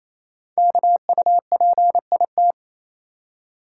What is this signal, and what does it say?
Morse code, heard as a keyed tone, about 29 wpm, 700 Hz. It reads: KUPIT
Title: Code Groups - 5 Letter:Numbers - 28wpm